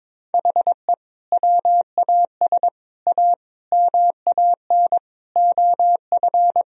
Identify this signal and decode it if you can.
Morse code, heard as a keyed tone, about 22 wpm, 700 Hz. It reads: HE WAS A MAN OF
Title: Sentences from Top 1000 Words - Review Part 3 - 22wpm